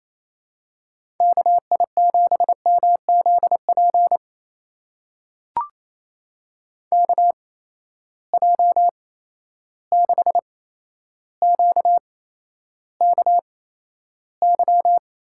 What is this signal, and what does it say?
Morse code, heard as a keyed tone, about 28 wpm, spaced as for 12 wpm, 700 Hz. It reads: KI7MZP KJ6QKY
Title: USA Call Signs - 8x Character Spacing - 28wpm